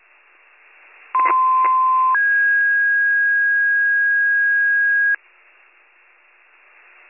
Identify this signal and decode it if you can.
Two-tone page, heard as a one-second tone, then a three-second tone, about 1030 then 1650 Hz. It is heard from 1.1 to 5.2 s.